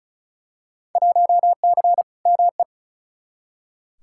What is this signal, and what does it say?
Morse code, heard as a keyed tone, about 36 wpm, 700 Hz. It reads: 1C ME
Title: Field Day Exchanges - 35wpm